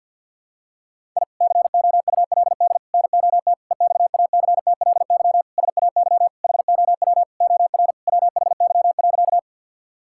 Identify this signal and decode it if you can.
Morse code, heard as a keyed tone, about 48 wpm, 700 Hz. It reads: I COULD NOT EXACTLY SAY HOW OR WHY.
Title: Sentences from Top 1000 Words - Review Part 6 - 50wpm